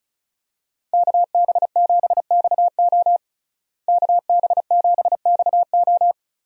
Morse code, heard as a keyed tone, about 36 wpm, 700 Hz. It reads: KB7XO KB7XO
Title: USA Call Signs - 5 Characters - Repeated 3x - 35wpm